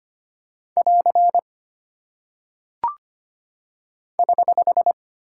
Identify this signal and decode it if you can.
Morse code, heard as a keyed tone, about 25 wpm, 700 Hz. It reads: <AR> <HH>
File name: Single Punctuation-Prosign - 25wpm